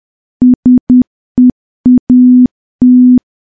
Morse code, heard as a keyed tone, about 10 wpm, 250 Hz. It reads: SEAT
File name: morse / morse-training-10WPM